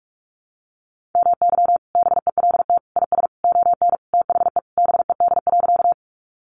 Morse code, heard as a keyed tone, about 44 wpm, 700 Hz. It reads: MY BELT IS ON THE BED.
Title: Sentences from Top 1400 Words 45wpm